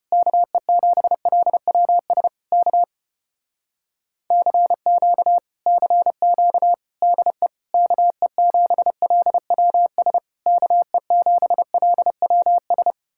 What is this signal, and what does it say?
Morse code, heard as a keyed tone, about 33 wpm, 700 Hz. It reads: KE7LWS K CQ CQ DE KE7LWS KE7LWS